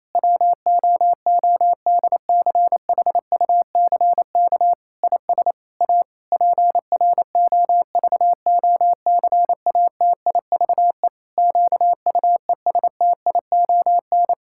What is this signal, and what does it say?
Morse code, heard as a keyed tone, about 28 wpm, 700 Hz. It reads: WOODCHUCK IS A PROVOCATIVE QUESTION
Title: Pangram Challenge v2 - 28wpm